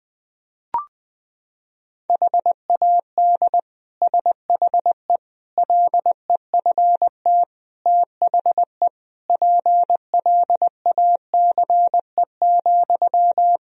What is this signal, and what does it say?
Morse code, heard as a keyed tone, about 21 wpm, 700 Hz. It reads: HAD SHE LEFT THE PLACE,
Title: Sentences from Top 1000 Words - Review Part 3 - 20wpm